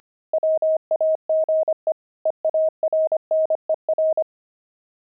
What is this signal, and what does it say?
Morse code, heard as a keyed tone, about 25 wpm, 630 Hz. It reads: WAGE EARNER
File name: pangram-fs-25-cs-25-zeal